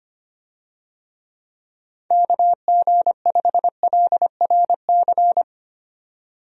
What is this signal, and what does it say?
Morse code, heard as a keyed tone, about 25 wpm, 700 Hz. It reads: KG5LRC
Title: USA Call Signs - 6 Characters - 25wpm